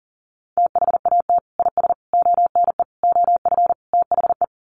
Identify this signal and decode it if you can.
Morse code, heard as a keyed tone, about 40 wpm, 700 Hz. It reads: THAT IS ONE OF THE
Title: Sets of 5 Words Encore 40wpm